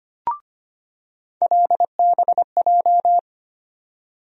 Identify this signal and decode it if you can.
Morse code, heard as a keyed tone, about 25 wpm, 700 Hz. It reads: LBJ
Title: USA Call Sign Postfixes - 3 Letters 25wpm